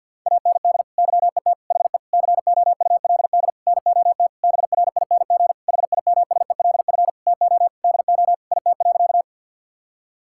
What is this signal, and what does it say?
Morse code, heard as a keyed tone, about 50 wpm, 700 Hz. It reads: AND YET HE COULD NOT BRING HIMSELF TO DO IT.
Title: Sentences from Top 1600 Words 50wpm